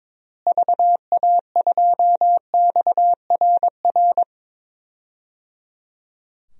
Morse code, heard as a keyed tone, about 22 wpm, 700 Hz. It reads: VA2XRR
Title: DXCC-100 Call Signs - 6 Characters - Repeated 3x - 22wpm